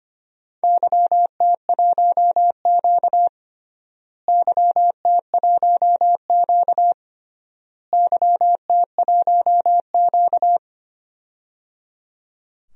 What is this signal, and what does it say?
Morse code, heard as a keyed tone, about 25 wpm, 700 Hz. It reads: YT1Q YT1Q YT1Q
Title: DXCC-100 Call Signs - 4 Characters - Repeated 3x - 25wpm